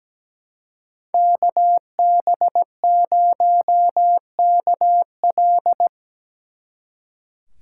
Morse code, heard as a keyed tone, about 20 wpm, 700 Hz. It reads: KB0KL
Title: Speed Racing - USA Call Signs 17wpm